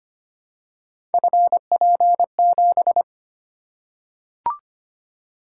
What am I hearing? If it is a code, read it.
Morse code, heard as a keyed tone, about 26 wpm, 700 Hz. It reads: FP7
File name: Three Letter-Number Combinations - 25wpm